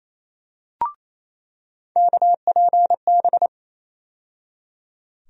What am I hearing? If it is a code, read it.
Morse code, heard as a keyed tone, about 28 wpm, 700 Hz. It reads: KPB
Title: USA Call Sign Postfixes - 3 Letters 28wpm